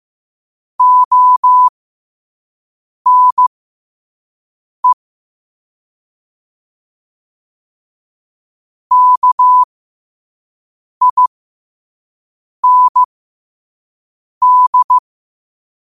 Morse code, heard as a keyed tone, about 15 wpm, spaced as for 5 wpm, 1000 Hz. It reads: ONE KIND
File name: qotdfiles / 5wpm